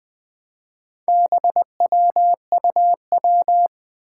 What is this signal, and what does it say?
Morse code, heard as a keyed tone, about 20 wpm, 700 Hz. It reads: BWUW
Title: Four Challenging Letter-Number Combinations 20wpm